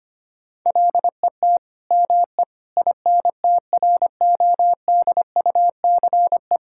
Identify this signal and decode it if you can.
Morse code, heard as a keyed tone, about 25 wpm, 700 Hz. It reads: LET ME INTRODUCE
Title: Sentences from Top 1600 Words 25wpm